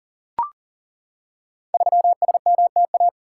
Morse code, heard as a keyed tone, about 40 wpm, 700 Hz. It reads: 3SMTA
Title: Code Groups - 5 Letter:Numbers - 40wpm